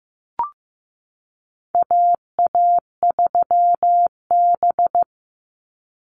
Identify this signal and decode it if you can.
Morse code, heard as a keyed tone, about 15 wpm, 700 Hz. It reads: AA3B
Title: Call Signs Repeated 2x - 15wpm